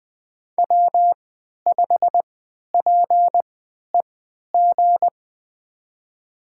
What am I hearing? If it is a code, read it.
Morse code, heard as a keyed tone, about 20 wpm, spaced as for 11 wpm, 700 Hz. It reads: W5PEG
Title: USA Call Signs - 3x Character Spacing - 20wpm